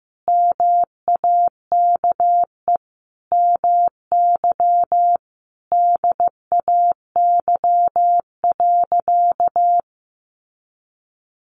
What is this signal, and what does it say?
Morse code, heard as a keyed tone, about 14 wpm, 700 Hz. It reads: MAKE MY DAY.
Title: Sentences from Top 1100 Words 15wpm